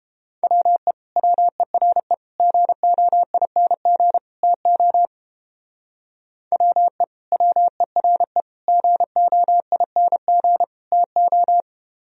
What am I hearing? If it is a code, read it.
Morse code, heard as a keyed tone, about 32 wpm, 700 Hz. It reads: WE WERE GOING TO WE WERE GOING TO